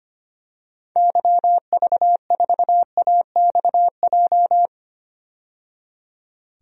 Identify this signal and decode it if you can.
Morse code, heard as a keyed tone, about 25 wpm, 700 Hz. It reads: YV4AXJ
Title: DXCC-100 Call Signs - 6 Characters - 25wpm